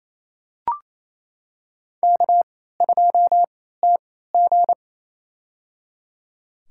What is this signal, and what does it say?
Morse code, heard as a keyed tone, about 28 wpm, spaced as for 16 wpm, 700 Hz. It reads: K2TG
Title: USA Call Signs - 4 Characters - 3x Character Spacing - 28wpm